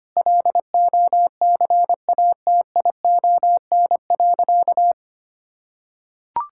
Morse code, heard as a keyed tone, about 25 wpm, 700 Hz. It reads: LOCATION.
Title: Sentences from Top 1000 Words - Review Part 6 - 25wpm